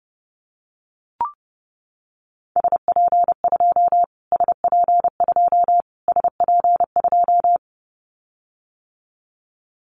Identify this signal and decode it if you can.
Morse code, heard as a keyed tone, about 28 wpm, 700 Hz. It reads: SP2 SP2 SP2
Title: Callsign Prefixes for Top 25 DXCC - Repeated 3x - 30wpm